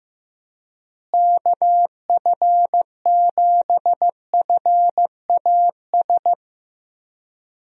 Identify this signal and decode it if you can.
Morse code, heard as a keyed tone, about 15 wpm, 700 Hz. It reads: KF7FAS
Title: USA Call Signs - 6 Characters - Repeated 2x - 15wpm